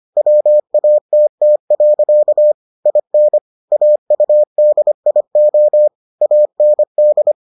Morse code, heard as a keyed tone, about 25 wpm, 600 Hz. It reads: WATT. IN AUDIO AND